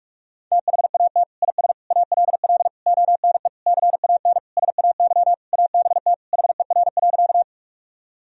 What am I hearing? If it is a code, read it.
Morse code, heard as a keyed tone, about 41 wpm, 700 Hz. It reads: THAT IS ALL ONE CAN SAY ABT HER.
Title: Sentences from Top 200 Words and Abbreviations 45wpm